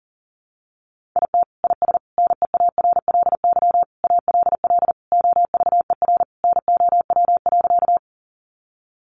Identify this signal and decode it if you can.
Morse code, heard as a keyed tone, about 40 wpm, 700 Hz. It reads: IT IS NEARLY ALL OVER NOW.
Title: Sentences from Top 1300 Words 40wpm